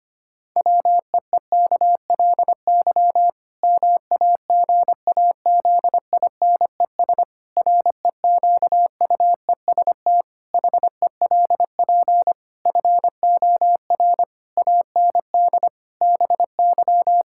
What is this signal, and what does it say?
Morse code, heard as a keyed tone, about 25 wpm, 700 Hz. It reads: WEEKLY MAGAZINES REQUEST HELP FOR AND BY
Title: Pangram Challenge v2 - 25wpm